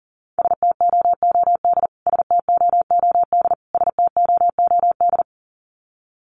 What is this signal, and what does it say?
Morse code, heard as a keyed tone, about 40 wpm, 700 Hz. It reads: STOOD STOOD STOOD
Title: Top 500 Words Repeated 40wpm